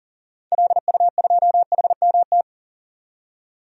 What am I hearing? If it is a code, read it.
Morse code, heard as a keyed tone, about 40 wpm, 700 Hz. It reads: LU2HMT
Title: DXCC-100 Call Signs - 6 Characters - 40wpm